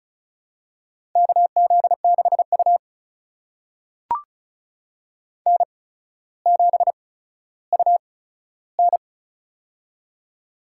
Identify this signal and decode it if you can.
Morse code, heard as a keyed tone, about 36 wpm, spaced as for 15 wpm, 700 Hz. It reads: KZ6U N7UN
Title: USA Call Signs - 4 Characters - 8x Character Spacing - 35wpm